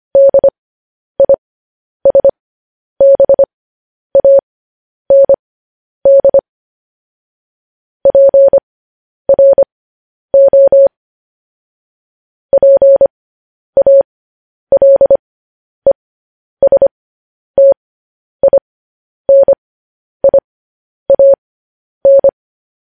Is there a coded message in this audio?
Yes: Morse code from 0.2 to 22.3 s, heard as a keyed tone, about 25 wpm, spaced as for 10 wpm, 550 Hz. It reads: DISBAND PRO PALESTINIAN